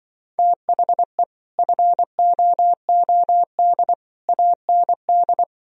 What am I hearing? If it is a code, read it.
Morse code, heard as a keyed tone, about 25 wpm, 700 Hz. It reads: THE FOOD AND